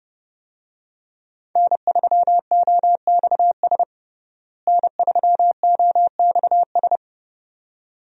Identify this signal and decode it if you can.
Morse code, heard as a keyed tone, about 30 wpm, 700 Hz. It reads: N3OXS N3OXS
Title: USA Call Signs - 5 Characters - Repeated 2x - 30wpm